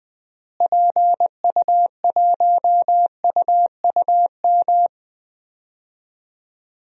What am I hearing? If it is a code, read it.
Morse code, heard as a keyed tone, about 20 wpm, 700 Hz. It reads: PU1UUM
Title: DXCC-100 Call Signs - 6 Characters - Repeated 2x - 20wpm